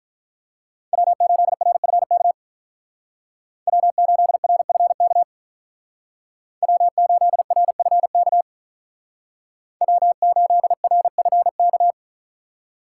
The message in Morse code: W8RFK W8RFK W8RFK W8RFK

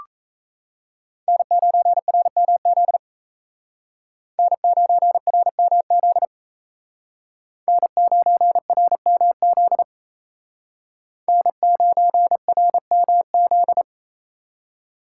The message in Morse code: N9RMZ N9RMZ N9RMZ N9RMZ

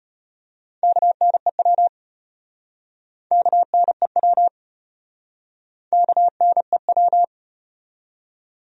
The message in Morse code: KNEW KNEW KNEW

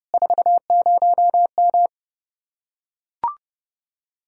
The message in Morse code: 40M

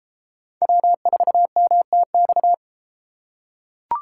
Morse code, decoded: W4MTX

